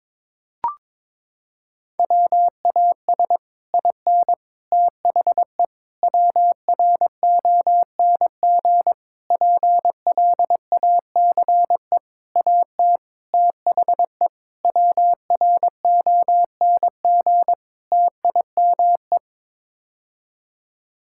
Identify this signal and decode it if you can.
Morse code, heard as a keyed tone, about 22 wpm, 700 Hz. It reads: WAS IN THE WRONG PLACE AT THE WRONG TIME
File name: Sets of 9 Words - 22wpm